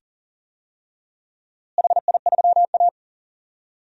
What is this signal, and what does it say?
Morse code, heard as a keyed tone, about 40 wpm, 700 Hz. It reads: HI3A
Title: DXCC-100 Call Signs - 4 Characters - Repeated 2x - 40wpm